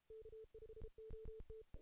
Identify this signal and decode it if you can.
Morse code, heard as a keyed tone, about 33 wpm, 440 Hz. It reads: K5OT
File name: 2016 JAN CW NAQP / 2016 Jan naqp k5ot - 0-8